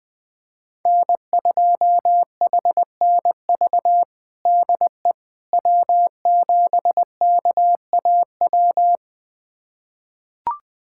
Morse code, heard as a keyed tone, about 21 wpm, 700 Hz. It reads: N2HNV DE W7KAW
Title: USA Callsign DE Callsign - 5 Characters - 20wpm